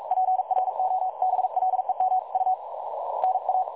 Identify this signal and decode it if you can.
Morse code, heard as a keyed tone, about 42 wpm, 750 Hz. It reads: PROBLEMU NA
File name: dlazaawansowanych / nagraniazpasma / websdr recording 3565.0kHz